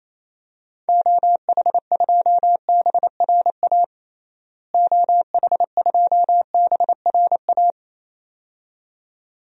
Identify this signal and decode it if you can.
Morse code, heard as a keyed tone, about 28 wpm, 700 Hz. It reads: OH2BRA OH2BRA
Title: DXCC-100 Call Signs - 6 Characters - Repeated 2x - 28wpm